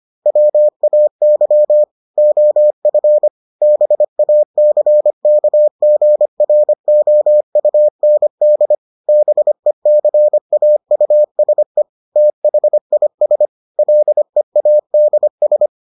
Morse code, heard as a keyed tone, about 25 wpm, 600 Hz. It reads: WAY OF BACKGROUND BECAUSE THIS LEADS